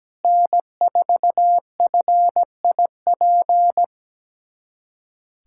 Morse code, heard as a keyed tone, about 17 wpm, 700 Hz. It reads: N4FIP
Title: USA Callsign DE Callsign - 5 Characters - 17wpm